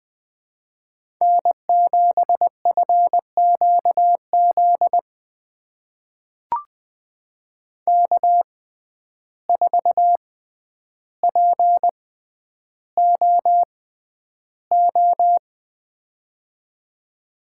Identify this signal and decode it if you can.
Morse code, heard as a keyed tone, about 20 wpm, spaced as for 10 wpm, 700 Hz. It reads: N7FQZ K4POO
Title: USA Call Signs - 5 Characters - 6x Character Spacing - 20wpm